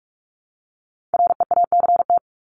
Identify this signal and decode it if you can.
Morse code, heard as a keyed tone, about 44 wpm, 700 Hz. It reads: REACT
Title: Top 1251-1500 Words 45wpm